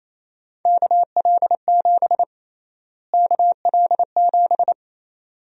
Morse code, heard as a keyed tone, about 27 wpm, 700 Hz. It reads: KL7 KL7
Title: USA Call Sign Prefixes - Repeated 2x - 28wpm